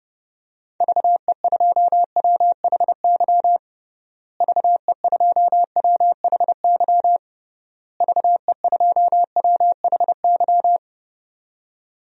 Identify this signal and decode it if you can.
Morse code, heard as a keyed tone, about 29 wpm, 700 Hz. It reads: VE2WHY VE2WHY VE2WHY
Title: DXCC-100 Call Signs - 6 Characters - Repeated 3x - 30wpm